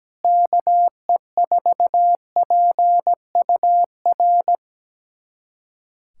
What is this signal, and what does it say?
Morse code, heard as a keyed tone, about 17 wpm, 700 Hz. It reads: KE4PUR
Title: USA Callsign DE Callsign - 6 Characters - 17wpm